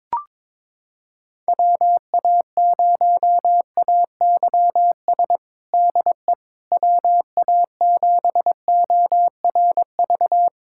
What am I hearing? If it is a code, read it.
Morse code, heard as a keyed tone, about 22 wpm, 700 Hz. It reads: WA0AYS DE WA7ORV
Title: USA Callsign DE Callsign - 6 Characters - 22wpm